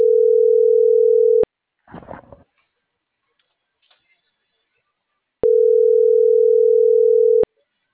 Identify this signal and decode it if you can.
Ringback tone (North America), 440 + 480 Hz.
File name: CENTRAL NMP / out patricia.martineznmp - .12959